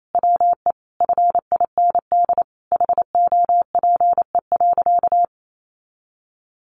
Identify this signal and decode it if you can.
Morse code, heard as a keyed tone, about 28 wpm, 700 Hz. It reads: WE FIND HOPE.
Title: Sentences from Top 1300 Words 28wpm